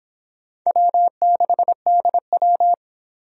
Morse code, heard as a keyed tone, about 26 wpm, 700 Hz. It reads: W6DW